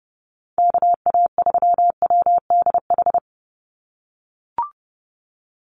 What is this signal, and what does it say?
Morse code, heard as a keyed tone, about 30 wpm, 700 Hz. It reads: KA3WDH
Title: USA Call Signs 30wpm